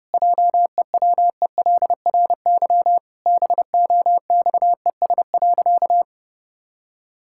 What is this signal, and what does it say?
Morse code, heard as a keyed tone, about 30 wpm, 700 Hz. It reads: JEWELRY BOXES.
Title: Pangram Challenge v2 - 30wpm